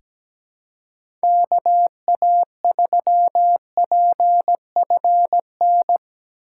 Morse code, heard as a keyed tone, about 17 wpm, 700 Hz. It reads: KA3PFN
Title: USA Call Signs - 2x Character Spacing - 17wpm